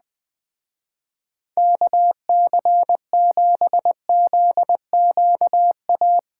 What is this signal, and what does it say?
Morse code, heard as a keyed tone, about 20 wpm, 700 Hz. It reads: KC7ZQA
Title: Speed Racing - USA 2x3 Call Signs 17wpm